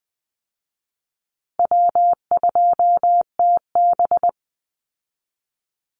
Morse code, heard as a keyed tone, about 20 wpm, 700 Hz. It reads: W2TB
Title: Call Signs Repeated 2x - 20wpm